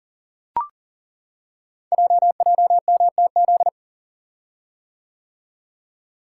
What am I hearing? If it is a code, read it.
Morse code, heard as a keyed tone, about 39 wpm, 690 Hz. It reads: JJMTZ
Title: Code Groups - 5 Letters - 40wpm